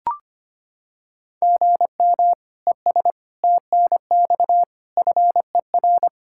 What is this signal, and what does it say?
Morse code, heard as a keyed tone, about 25 wpm, 700 Hz. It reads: GM ES TNX FER
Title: Practice QSO - Exchange 1 - Beginning - 25wpm